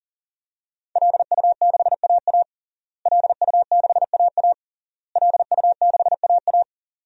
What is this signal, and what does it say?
Morse code, heard as a keyed tone, about 40 wpm, 700 Hz. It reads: LU6AA LU6AA LU6AA
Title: DXCC-100 Call Signs - 5 Characters - Repeated 3x - 40wpm